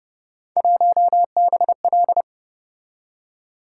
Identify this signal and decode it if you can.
Morse code, heard as a keyed tone, about 30 wpm, 700 Hz. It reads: 1BL